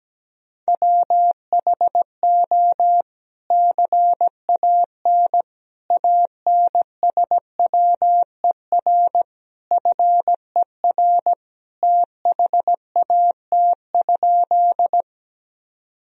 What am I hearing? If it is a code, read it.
Morse code, heard as a keyed tone, about 17 wpm, 700 Hz. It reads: WHO CAN ANSWER FER THAT?
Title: Sentences from Top 300 Words and Abbreviations 17wpm